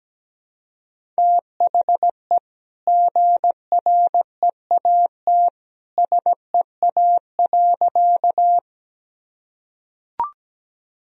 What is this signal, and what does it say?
Morse code, heard as a keyed tone, about 17 wpm, 700 Hz. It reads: THE GREAT SEA.
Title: Sentences from Top 1000 Words - Review Part 1 - 17wpm